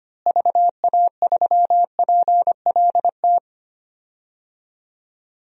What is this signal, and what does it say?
Morse code, heard as a keyed tone, about 25 wpm, 700 Hz. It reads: VA3PLT
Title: DXCC-100 Call Signs - 6 Characters - Repeated 3x - 25wpm